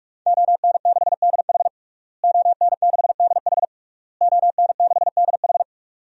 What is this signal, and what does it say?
Morse code, heard as a keyed tone, about 46 wpm, 700 Hz. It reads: ON6DH ON6DH ON6DH
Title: DXCC-100 Call Signs - 5 Characters - Repeated 3x - 45wpm